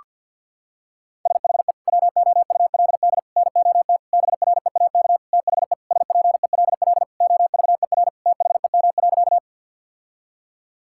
Morse code, heard as a keyed tone, about 50 wpm, 700 Hz. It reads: SHE WOULD NOT BREAK THE SPELL OVER THEM.
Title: Sentences from Top 1000 Words - Review Part 6 - 50wpm